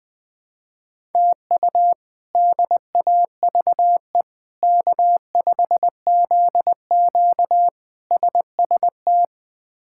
Morse code, heard as a keyed tone, about 20 wpm, 700 Hz. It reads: TU DAVE K5ZQ SST
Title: K1USN SST QSO Practice - 20wpm